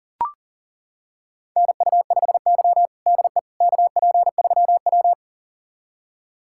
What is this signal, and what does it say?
Morse code, heard as a keyed tone, about 40 wpm, 700 Hz. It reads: NU5Y DE KP3W
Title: USA Callsign DE Callsign - 4 Characters - 40wpm